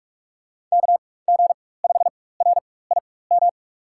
Morse code, heard as a keyed tone, about 41 wpm, spaced as for 20 wpm, 700 Hz. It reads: KG5RIM
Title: USA Call Signs - 6 Characters - 4x Character Spacing - 45wpm